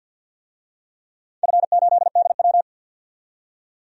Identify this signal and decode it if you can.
Morse code, heard as a keyed tone, about 51 wpm, 700 Hz. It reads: F8DW
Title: DXCC-100 Call Signs - 4 Characters - 50wpm